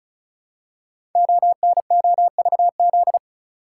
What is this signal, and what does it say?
Morse code, heard as a keyed tone, about 36 wpm, 700 Hz. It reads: ONOVZ
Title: Code Groups - 5 Letters - 35wpm